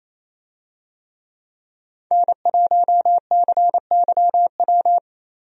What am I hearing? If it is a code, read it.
Morse code, heard as a keyed tone, about 28 wpm, 700 Hz. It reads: N1CYW
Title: USA Call Signs - 5 Characters - 28wpm